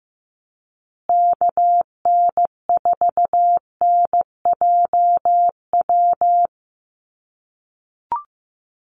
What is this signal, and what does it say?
Morse code, heard as a keyed tone, about 15 wpm, 700 Hz. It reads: KN4NJW
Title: USA Call Signs Repeated 2x 15wpm